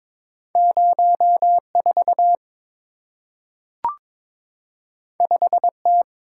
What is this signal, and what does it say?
Morse code, heard as a keyed tone, about 22 wpm, 700 Hz. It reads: 04 5T
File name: practice / 22wpm / Two Letter-Number Combinations - 22wpm